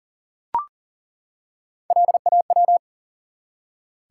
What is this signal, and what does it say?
Morse code, heard as a keyed tone, about 41 wpm, 700 Hz. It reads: LAW
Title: Three Letter Words - 40wpm